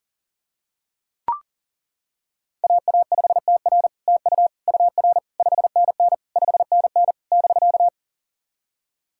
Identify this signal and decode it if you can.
Morse code, heard as a keyed tone, about 39 wpm, 700 Hz. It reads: AA5TR TU UR 5NN 5NN <BK>
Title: SOTA QSO Practice 40wpm